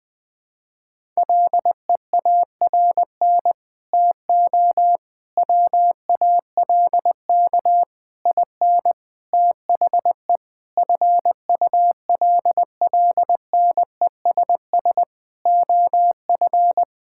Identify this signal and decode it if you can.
Morse code, heard as a keyed tone, about 20 wpm, 700 Hz. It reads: LEARN TO WALK IN THE FULLNESS OF
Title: Sentences with 9-Word Phrases - 20wpm